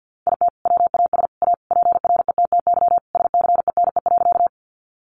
Morse code, heard as a keyed tone, about 50 wpm, 700 Hz. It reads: IT WAS A PRETTY SCENE.
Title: Sentences from Top 1100 Words 50wpm